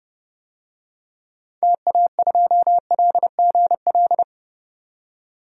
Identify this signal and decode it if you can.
Morse code, heard as a keyed tone, about 32 wpm, 700 Hz. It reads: TA2LGL
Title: DXCC-100 Call Signs - 6 Characters - 30wpm